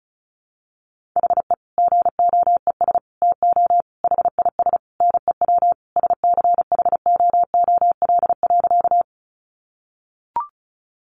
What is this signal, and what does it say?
Morse code, heard as a keyed tone, about 34 wpm, 700 Hz. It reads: HE GOES TO HIS NEW SCHOOL.
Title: Easy Phrases 35wpm